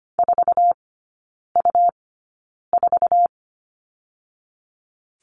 Morse code, heard as a keyed tone, about 25 wpm, spaced as for 9 wpm, 700 Hz. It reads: 4U4
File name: Lesson 17e - Letter u - Three Characters 25wpm